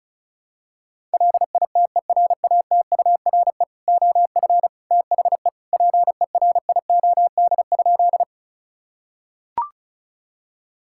Morse code, heard as a keyed tone, about 36 wpm, 700 Hz. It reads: LITERATURE OF THE PERIOD?
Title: Sentences from Top 1000 Words - Review Part 5 - 35wpm